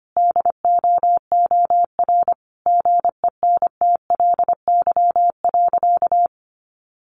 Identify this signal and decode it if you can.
Morse code, heard as a keyed tone, about 25 wpm, 700 Hz. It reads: DOOR GENTLY.
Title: Sentences from Top 1500 Words 25wpm